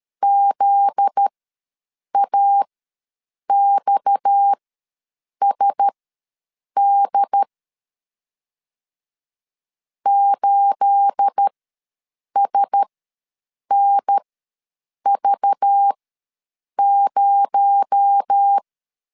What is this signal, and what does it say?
Morse code, heard as a keyed tone, about 13 wpm, spaced as for 7 wpm, 800 Hz. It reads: ZAXSD 8SNV0
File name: LEZIONI / 9 28 ws 12wpm 3354